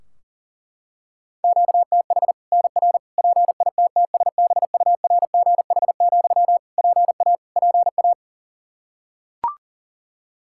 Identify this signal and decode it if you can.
Morse code, heard as a keyed tone, about 40 wpm, 700 Hz. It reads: QTH NR PITTSBURGH, PA PA
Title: Practice QSO - Exchange 1 - Beginning - 40wpm